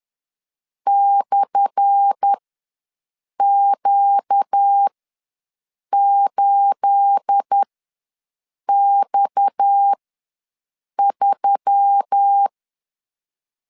Morse code, heard as a keyed tone, about 10 wpm, spaced as for 6 wpm, 800 Hz. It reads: /Q8X3
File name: LEZIONI / 10 SIGNS / 10A SLASH ws 10wpm 4121